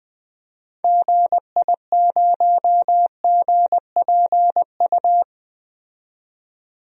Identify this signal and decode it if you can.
Morse code, heard as a keyed tone, about 20 wpm, 700 Hz. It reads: GI0GPU
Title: DXCC-100 Call Signs - 6 Characters - 20wpm